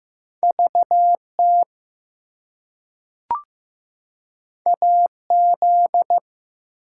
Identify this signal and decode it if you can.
Morse code, heard as a keyed tone, about 15 wpm, 700 Hz. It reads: VT AZ
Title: US States and Canadian Provinces 15wpm